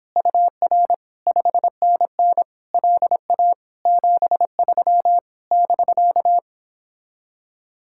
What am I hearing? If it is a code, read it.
Morse code, heard as a keyed tone, about 30 wpm, 700 Hz. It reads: UR 5NN LA 73 <BK>